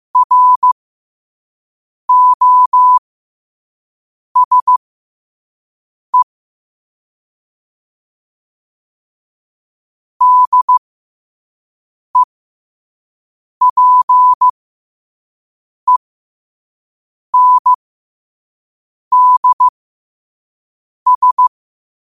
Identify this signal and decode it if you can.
Morse code, heard as a keyed tone, about 15 wpm, spaced as for 5 wpm, 1000 Hz. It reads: ROSE DEPENDS
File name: qotdfiles / 5wpm